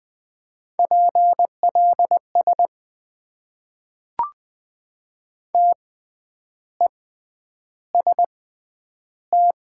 Morse code, heard as a keyed tone, about 20 wpm, spaced as for 9 wpm, 700 Hz. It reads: PLS TEST
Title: Top 90 QSO Elements - 6x Character Spacing - 20wpm